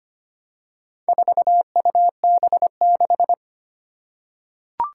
Morse code, heard as a keyed tone, about 25 wpm, 700 Hz. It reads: 4UB6
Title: Four Challenging Letter-Number Combinations 25wpm